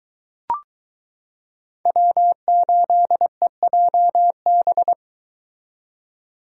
Morse code, heard as a keyed tone, about 22 wpm, 700 Hz. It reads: W8EJB